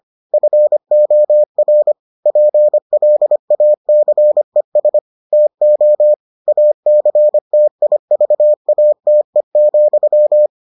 Morse code, heard as a keyed tone, about 25 wpm, 600 Hz. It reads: FOR PLACES TO ACTIVATE,